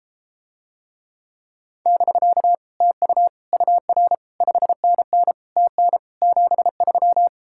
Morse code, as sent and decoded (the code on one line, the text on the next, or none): -...-.- / - ..- / ..- .-. / ..... -. -. / - -. / --... ...--
<BK> TU UR 5NN TN 73